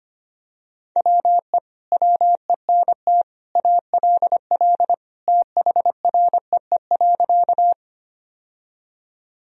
Morse code, heard as a keyed tone, about 25 wpm, 700 Hz. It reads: WE WENT ALL THREE.
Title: Sentences from Top 1000 Words - Review Part 1 - 25wpm